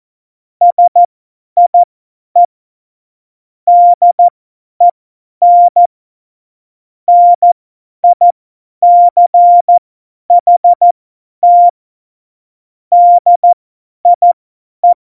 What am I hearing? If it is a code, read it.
Morse code, heard as a keyed tone, about 14 wpm, spaced as for 10 wpm, 700 Hz. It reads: SIE DEN NICHT DIE
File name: Morsekurs / Witze / Witz 24. September 2019, 50 F70